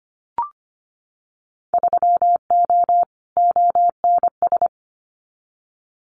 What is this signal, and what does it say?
Morse code, heard as a keyed tone, about 25 wpm, 700 Hz. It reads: 3O ONS
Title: Winter Field Day Exchanges 25wpm